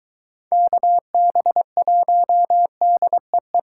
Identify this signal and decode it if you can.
Morse code, heard as a keyed tone, about 23 wpm, 700 Hz. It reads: KB1DEE